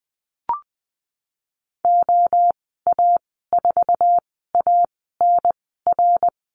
Morse code, heard as a keyed tone, about 20 wpm, spaced as for 15 wpm, 700 Hz. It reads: OA4ANR
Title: 055 Lesson 10g - Letter r - Call Signs 20wpm